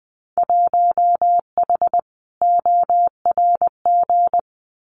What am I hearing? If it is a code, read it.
Morse code, heard as a keyed tone, about 20 wpm, 700 Hz. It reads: 1H ORG
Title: Winter Field Day Exchanges 20wpm